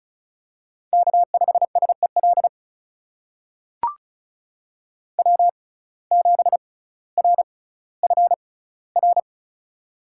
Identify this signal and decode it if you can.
Morse code, heard as a keyed tone, about 36 wpm, spaced as for 18 wpm, 700 Hz. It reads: K5SEL W7RFR
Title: USA Call Signs - 5 Characters - 6x Character Spacing - 35wpm